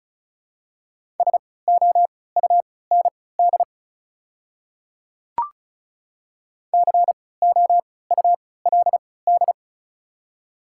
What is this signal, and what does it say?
Morse code, heard as a keyed tone, about 36 wpm, spaced as for 20 wpm, 700 Hz. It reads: SOUND COULD
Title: 3x Character Spacing - Top 100 Words - 35wpm